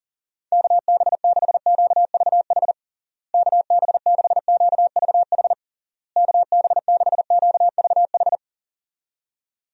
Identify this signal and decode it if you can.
Morse code, heard as a keyed tone, about 40 wpm, 700 Hz. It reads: KB6QVH KB6QVH KB6QVH
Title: USA Call Signs - 6 Characters - Repeated 3x - 40wpm